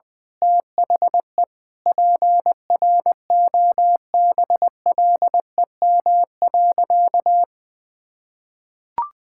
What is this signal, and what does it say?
Morse code, heard as a keyed tone, about 20 wpm, 700 Hz. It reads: THE PROBLEM.
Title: Sentences with 9-Word Phrases - 20wpm